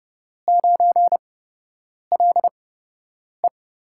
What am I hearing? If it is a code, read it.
Morse code, heard as a keyed tone, about 29 wpm, spaced as for 8 wpm, 700 Hz. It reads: 9LE
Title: USA Call Signs - 8x Character Spacing - 30wpm